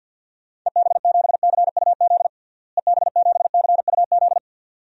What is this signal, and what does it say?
Morse code, heard as a keyed tone, about 53 wpm, 700 Hz. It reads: EB7CUZ EB7CUZ
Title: DXCC-100 Call Signs - 6 Characters - Repeated 2x - 50wpm